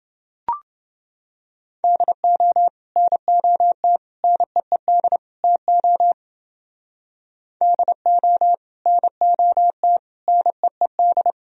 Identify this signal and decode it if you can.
Morse code, heard as a keyed tone, about 29 wpm, 700 Hz. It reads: DO NOT NEED TO DO NOT NEED